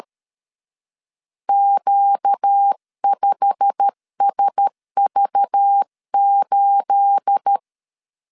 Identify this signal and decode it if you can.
Morse code, heard as a keyed tone, about 13 wpm, 800 Hz. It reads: Q5SV8
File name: LEZIONI / 9 28 12wpm 2351